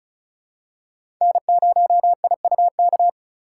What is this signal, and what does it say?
Morse code, heard as a keyed tone, about 34 wpm, 700 Hz. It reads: N0IUK
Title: USA Call Signs - 4x Character Spacing - 35wpm